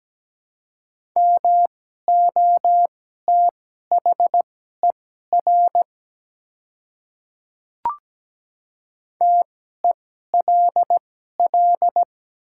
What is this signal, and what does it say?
Morse code, heard as a keyed tone, about 17 wpm, spaced as for 12 wpm, 700 Hz. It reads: MOTHER TELL
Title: 2x Character Spacing - Top 300 Words - 17wpm